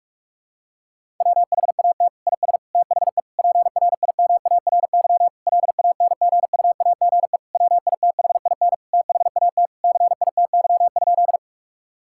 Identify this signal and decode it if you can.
Morse code, heard as a keyed tone, about 46 wpm, 700 Hz. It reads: WHAT IS THE PRIMARY LANGUAGE WITHIN THAT CITY?
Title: Sentences from Top 1000 Words - Review Part 5 - 45wpm